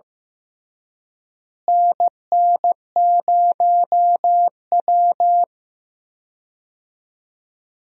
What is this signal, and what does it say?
Morse code, heard as a keyed tone, about 15 wpm, 700 Hz. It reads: NN0W
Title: USA Call Signs - 4 Characters - Repeated 3x - 15wpm